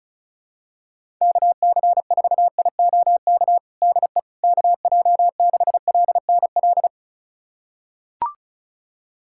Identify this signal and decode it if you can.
Morse code, heard as a keyed tone, about 36 wpm, 700 Hz. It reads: KC4IOK DE KJ6LNL
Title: USA Callsign DE Callsign - 6 Characters - 35wpm